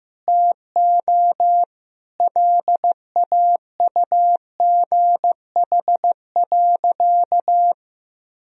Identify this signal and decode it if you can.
Morse code, heard as a keyed tone, about 15 wpm, 700 Hz. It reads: TO LAUGH.
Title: Sentences from Top 1000 Words - Review Part 1 - 15wpm